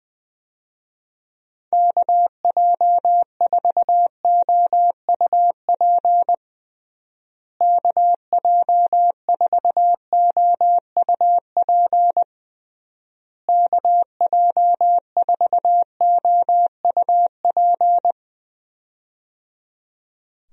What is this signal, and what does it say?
Morse code, heard as a keyed tone, about 20 wpm, 700 Hz. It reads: KJ4OUP KJ4OUP KJ4OUP
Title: USA Call Signs - 6 Characters - Repeated 3x - 20wpm